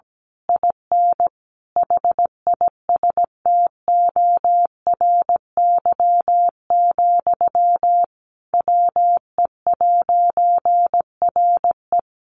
Morse code, heard as a keyed tone, about 17 wpm, 700 Hz. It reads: IN HISTORY, WE'RE
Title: A Sound Of Thunder 17wpm